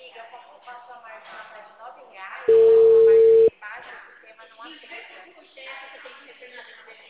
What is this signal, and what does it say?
Ringback tone (China, 450 Hz), with speech.
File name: out- -2046- - - .26779